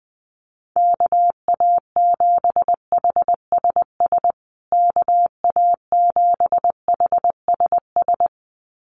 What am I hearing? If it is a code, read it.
Morse code, heard as a keyed tone, about 20 wpm, 700 Hz. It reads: KA7HSS KA7HSS
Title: USA Call Signs Repeated 2x 20wpm